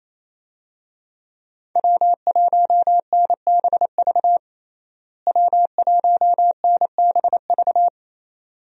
Morse code, heard as a keyed tone, about 28 wpm, 700 Hz. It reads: W1NBV W1NBV
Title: USA Call Signs - 5 Characters - Repeated 2x - 28wpm